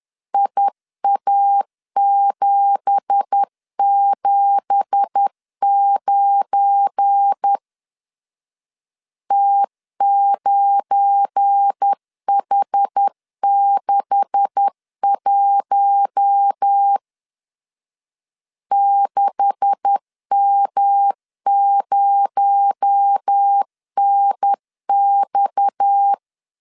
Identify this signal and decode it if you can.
Morse code, heard as a keyed tone, about 10 wpm, 800 Hz. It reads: IA779 T9H61 6M0NX
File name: LEZIONI / 9 2867 10wpm 2733